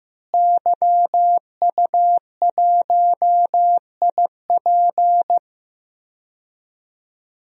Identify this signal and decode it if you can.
Morse code, heard as a keyed tone, about 15 wpm, 700 Hz. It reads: YU1IP
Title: DXCC-100 Call Signs - 5 Characters - Repeated 3x - 15wpm